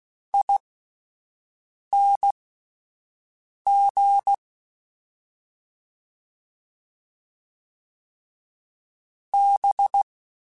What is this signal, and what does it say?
Morse code, heard as a keyed tone, about 16 wpm, spaced as for 5 wpm, 800 Hz. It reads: ING B
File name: Misc / Misc-mp3s / Jumps @5